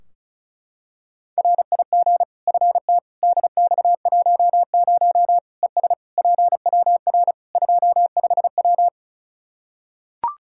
Morse code, heard as a keyed tone, about 34 wpm, 700 Hz. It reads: RIG FT DX10 ES PWR 25W
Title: Practice QSO - Exchange 2 - Info - 35wpm